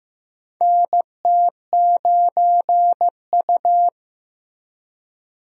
Morse code, heard as a keyed tone, about 15 wpm, 700 Hz. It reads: NT9U
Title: USA Call Signs - 4 Characters - Repeated 3x - 15wpm